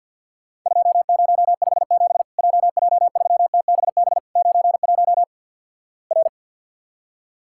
Morse code, heard as a keyed tone, about 50 wpm, 700 Hz. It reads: 2057 JJ3TBB 91 R
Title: Worked All Europe DX Contest - QTC Exchange - 50wpm